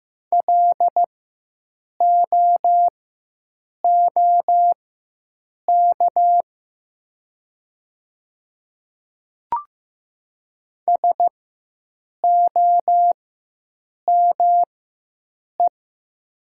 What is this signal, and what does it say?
Morse code, heard as a keyed tone, about 15 wpm, spaced as for 7 wpm, 700 Hz. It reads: LOOK SOME
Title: 4x Character Spacing - Top 100 Words - 15wpm